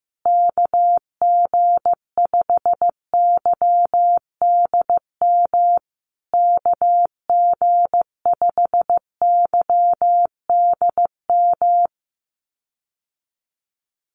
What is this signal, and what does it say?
Morse code, heard as a keyed tone, about 14 wpm, 700 Hz. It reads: KG5YDM KG5YDM
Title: USA Call Signs Repeated 3x 15wpm